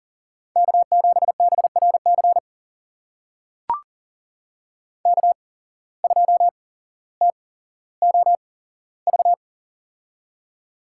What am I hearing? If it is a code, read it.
Morse code, heard as a keyed tone, about 37 wpm, spaced as for 17 wpm, 700 Hz. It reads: K7BRC K2TOV
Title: USA Call Signs - 5 Characters - 8x Character Spacing - 40wpm